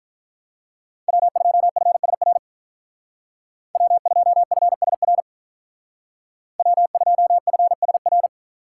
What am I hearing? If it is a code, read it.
Morse code, heard as a keyed tone, about 47 wpm, 700 Hz. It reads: W2FSR W2FSR W2FSR